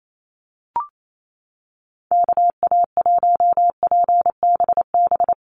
Morse code, heard as a keyed tone, about 28 wpm, 700 Hz. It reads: KA1PBB
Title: USA Call Signs 28wpm